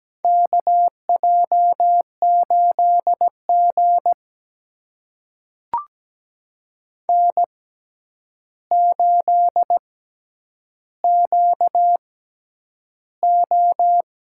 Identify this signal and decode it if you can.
Morse code, heard as a keyed tone, about 17 wpm, spaced as for 9 wpm, 700 Hz. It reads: KJ8G N8QO
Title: USA Call Signs - 4 Characters - 6x Character Spacing - 17wpm